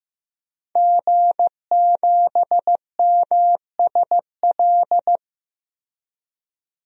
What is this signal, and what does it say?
Morse code, heard as a keyed tone, about 15 wpm, 700 Hz. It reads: G7MSL
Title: DXCC-100 Call Signs - 5 Characters - 15wpm